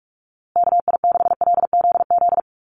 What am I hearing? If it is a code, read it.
Morse code, heard as a keyed tone, about 44 wpm, 700 Hz. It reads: KI6LZZ
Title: USA Call Signs Repeated 3x 45wpm